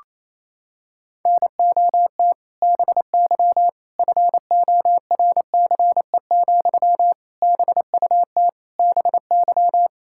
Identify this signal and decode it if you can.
Morse code, heard as a keyed tone, about 28 wpm, 700 Hz. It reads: NOT BY FORCE, BUT BY
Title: Sentences from Top 1000 Words - Review Part 5 - 28wpm